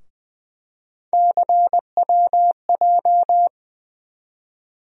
Morse code, heard as a keyed tone, about 21 wpm, 700 Hz. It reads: CWJ
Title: USA Call Sign Postfixes - 3 Letters 20wpm